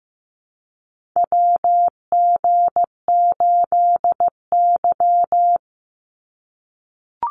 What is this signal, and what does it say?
Morse code, heard as a keyed tone, about 15 wpm, 700 Hz. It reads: WG8Y
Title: NA SOTA Chaser Callsigns 15wpm